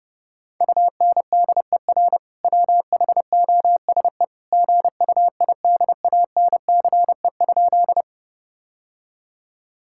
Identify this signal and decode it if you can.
Morse code, heard as a keyed tone, about 30 wpm, 700 Hz. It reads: UNDER WHOSE GUIDANCE?
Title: Sentences from Top 1600 Words 30wpm